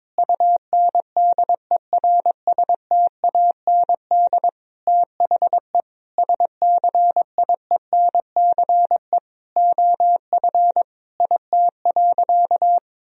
Morse code, heard as a keyed tone, about 21 wpm, 700 Hz. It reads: UNDERSTAND THE SCIENCE OF IT.
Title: Sentences from Top 1000 Words - Review Part 2 - 22wpm